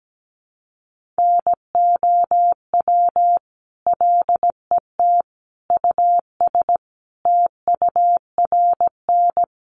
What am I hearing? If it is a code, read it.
Morse code, heard as a keyed tone, about 17 wpm, 700 Hz. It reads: NOW LET US TURN
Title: Sentences from Top 1400 Words 17wpm